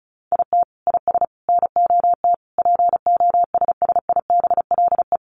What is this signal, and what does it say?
Morse code, heard as a keyed tone, about 34 wpm, 700 Hz. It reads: IT IS NOT POSSIBLE
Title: Sets of 5 Words Encore2 35wpm